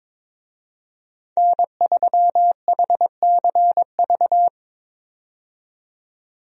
Morse code, heard as a keyed tone, about 23 wpm, 700 Hz. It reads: N3HCV
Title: USA Call Signs - 5 Characters - 22wpm